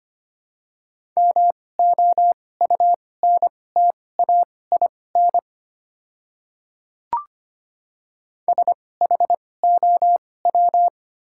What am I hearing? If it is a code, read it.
Morse code, heard as a keyed tone, about 25 wpm, spaced as for 18 wpm, 700 Hz. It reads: MOUNTAIN SHOW
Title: 2x Character Spacing - Top 500 Words - 25wpm